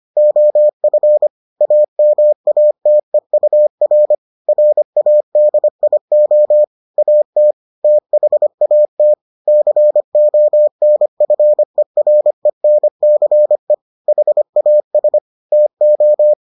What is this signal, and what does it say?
Morse code, heard as a keyed tone, about 25 wpm, 600 Hz. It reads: OF AMATEUR RADIO AT THAT CONFERENCE HAS TO